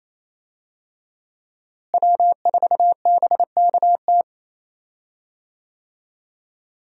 Morse code, heard as a keyed tone, about 28 wpm, 700 Hz. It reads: W4BKT
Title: USA Call Signs - 5 Characters - Repeated 2x - 28wpm